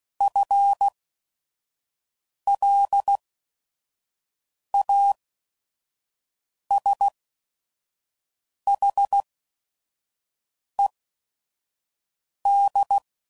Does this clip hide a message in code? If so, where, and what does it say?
Yes: Morse code from 0.2 to 13.0 s, heard as a keyed tone, about 16 wpm, spaced as for 5 wpm, 800 Hz. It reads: FLASHED